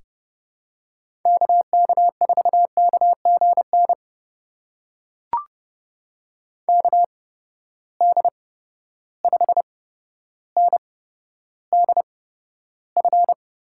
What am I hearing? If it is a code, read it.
Morse code, heard as a keyed tone, about 30 wpm, spaced as for 13 wpm, 700 Hz. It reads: KK4KGN KD5NDF
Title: USA Call Signs - 6 Characters - 8x Character Spacing - 30wpm